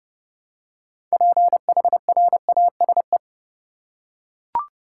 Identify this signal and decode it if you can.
Morse code, heard as a keyed tone, about 30 wpm, 700 Hz. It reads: PHRASE
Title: Six Letter Words - 30wpm